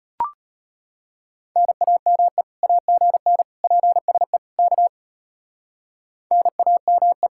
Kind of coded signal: Morse code

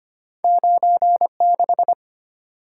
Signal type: Morse code

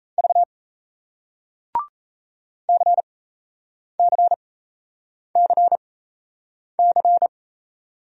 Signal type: Morse code

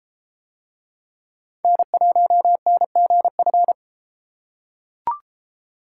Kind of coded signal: Morse code

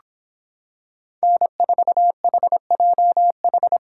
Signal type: Morse code

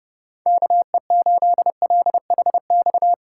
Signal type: Morse code

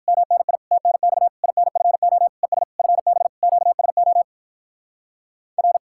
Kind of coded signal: Morse code

